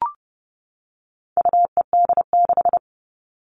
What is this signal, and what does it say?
Morse code, heard as a keyed tone, about 30 wpm, 700 Hz. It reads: UED6
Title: All Cut Numbers - 4-Digit 30wpm